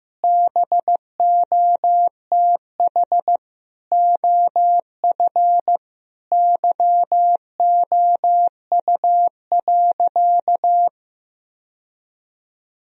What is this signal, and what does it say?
Morse code, heard as a keyed tone, about 15 wpm, 700 Hz. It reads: BOTH OF YOU.
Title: Sentences from Top 1000 Words - Review Part 1 - 15wpm